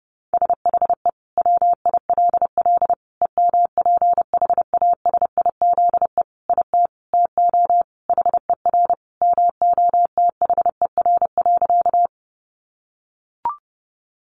Morse code, heard as a keyed tone, about 30 wpm, 700 Hz. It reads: SHE WILL EMPHASIZE IT TO HER MOTHER.
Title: Sentences from Top 1500 Words 30wpm